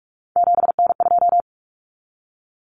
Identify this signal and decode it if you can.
Morse code, heard as a keyed tone, about 44 wpm, 700 Hz. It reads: 7N2
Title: Common Cut Numbers - 3-Digit 45wpm